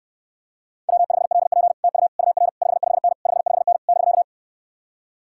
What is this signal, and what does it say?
Morse code, heard as a keyed tone, about 69 wpm, 700 Hz. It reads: W5LP TU UR 55N 55N <BK>